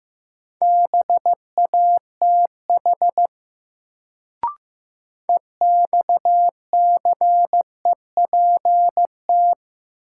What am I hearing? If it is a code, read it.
Morse code, heard as a keyed tone, about 15 wpm, 700 Hz. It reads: BATH EXCEPT
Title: Sentences from Top 1600 Words - New Word Drill Alt - 15wpm